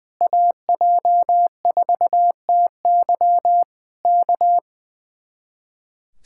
Morse code, heard as a keyed tone, about 20 wpm, 700 Hz. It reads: AJ4TY K
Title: Speed-Racing Common QSO Phrases - 20wpm